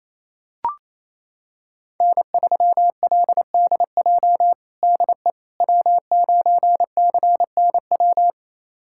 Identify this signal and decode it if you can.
Morse code, heard as a keyed tone, about 28 wpm, 700 Hz. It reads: N3LDJ DE W9CNW
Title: USA Callsign DE Callsign - 5 Characters - 28wpm